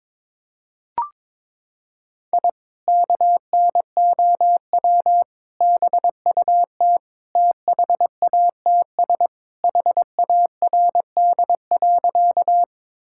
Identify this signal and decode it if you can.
Morse code, heard as a keyed tone, about 22 wpm, 700 Hz. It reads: I KNOW BUT THATS HARD.
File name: practice / 22wpm / Sentences with 2-Word Phrases - 22wpm